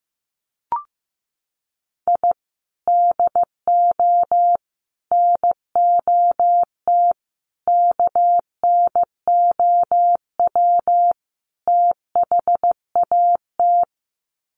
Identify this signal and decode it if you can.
Morse code, heard as a keyed tone, about 15 wpm, 700 Hz. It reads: I DO NOT KNOW THAT
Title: Sets of 5 Words 15wpm